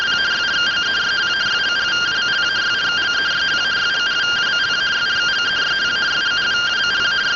Two-tone teleprinter signal, about 170 Hz shift.